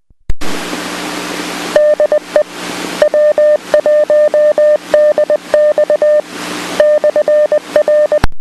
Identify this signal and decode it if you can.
Morse code, heard as a keyed tone, about 20 wpm, 600 Hz. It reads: DE W1DX /R